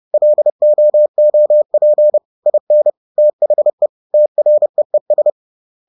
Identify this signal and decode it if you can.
Morse code, heard as a keyed tone, about 30 wpm, 600 Hz. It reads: LOOP IN THE TREES